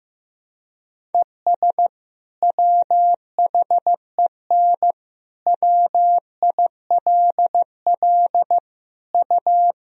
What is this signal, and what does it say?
Morse code, heard as a keyed tone, about 15 wpm, 700 Hz. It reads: ES WHEN WILL U
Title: Sentences from Top 100 Words and Abbreviations 15wpm